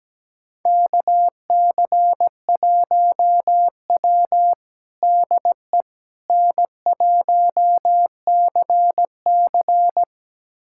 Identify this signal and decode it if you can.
Morse code, heard as a keyed tone, about 17 wpm, 700 Hz. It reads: KC1W DE N1CC
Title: USA Callsign DE Callsign - 4 Characters - 17wpm